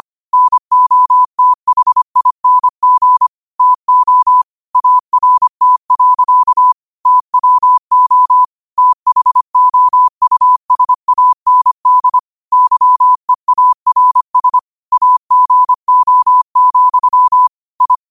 Morse code, heard as a keyed tone, about 25 wpm, 1000 Hz. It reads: NOTHING TO ART. TWO THOUSAND YEARS AGO, I
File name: qotdfiles / 25wpm